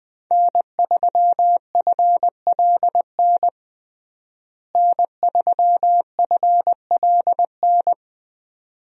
Morse code, heard as a keyed tone, about 20 wpm, 700 Hz. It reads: N3FLN N3FLN
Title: USA Call Signs - 5 Characters - Repeated 2x - 20wpm